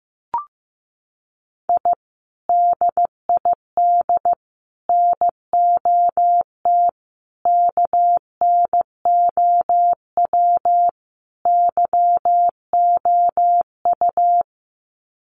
Morse code, heard as a keyed tone, about 15 wpm, 700 Hz. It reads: I DID NOT KNOW YOU
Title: Sets of 5 Words Encore2 15wpm